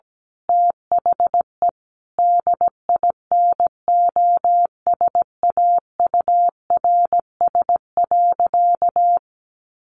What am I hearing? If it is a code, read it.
Morse code, heard as a keyed tone, about 17 wpm, 700 Hz. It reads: THE DINOSAURS.
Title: A Sound Of Thunder 17wpm